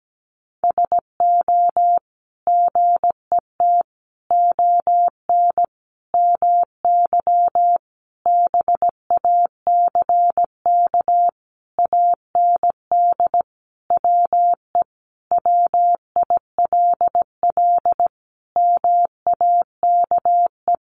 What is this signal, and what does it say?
Morse code, heard as a keyed tone, about 17 wpm, 700 Hz. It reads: SO GET ON MY BACK AND WE WILL MAKE
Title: Sentences from Top 1000 Words 17wpm